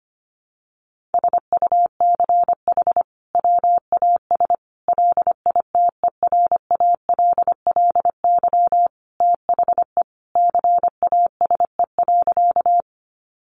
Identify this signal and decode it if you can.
Morse code, heard as a keyed tone, about 25 wpm, 700 Hz. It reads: SUCH WAS LITERALLY THE CASE.
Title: Sentences from Top 1400 Words 25wpm